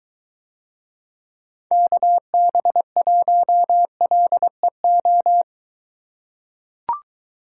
Morse code, heard as a keyed tone, about 22 wpm, 700 Hz. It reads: KB1LEO